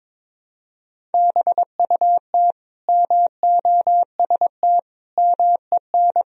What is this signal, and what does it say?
Morse code, heard as a keyed tone, about 22 wpm, 700 Hz. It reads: BUT MOST MEN
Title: Sentences from Top 1000 Words - Review Part 1 - 22wpm